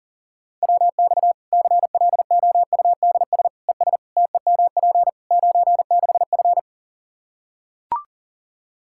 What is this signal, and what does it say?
Morse code, heard as a keyed tone, about 40 wpm, 700 Hz. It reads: WX CLOUDS ES TEMP 96F
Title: Practice QSO - Exchange 2 - Info - 40wpm